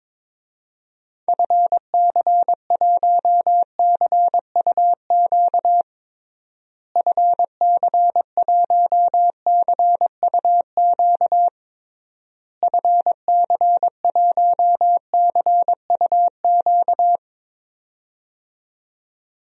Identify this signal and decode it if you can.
Morse code, heard as a keyed tone, about 22 wpm, 700 Hz. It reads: FC1CUQ FC1CUQ FC1CUQ
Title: DXCC-100 Call Signs - 6 Characters - Repeated 3x - 22wpm